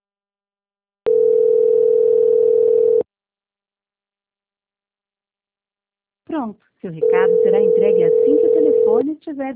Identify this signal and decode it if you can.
Ringback tone (North America), 440 + 480 Hz, with speech.
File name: out- -2002- - - .446